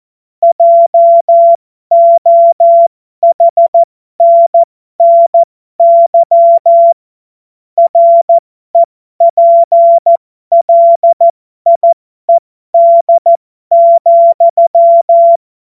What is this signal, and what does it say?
Morse code, heard as a keyed tone, about 14 wpm, spaced as for 12 wpm, 680 Hz. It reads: JOHNNY REPLIED,
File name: Morsekurs / Witze